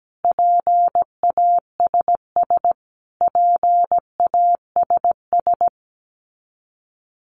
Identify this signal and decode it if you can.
Morse code, heard as a keyed tone, about 17 wpm, 700 Hz. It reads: PASS PASS
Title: Top 500 Words Repeated 17wpm